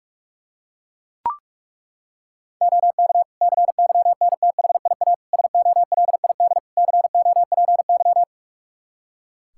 Morse code, heard as a keyed tone, about 41 wpm, 700 Hz. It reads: OK CYNTHIA SOLID COPY
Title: K1USN SST QSO Practice 45wpm